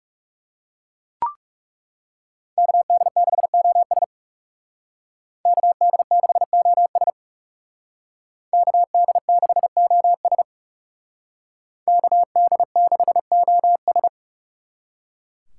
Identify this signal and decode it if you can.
Morse code, heard as a keyed tone, about 40 wpm, 700 Hz. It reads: KD6OS KD6OS KD6OS KD6OS